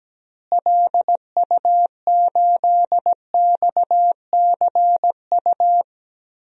Morse code, heard as a keyed tone, about 17 wpm, 700 Hz. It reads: LU8XCU
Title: DXCC-100 Call Signs - 6 Characters - Repeated 2x - 17wpm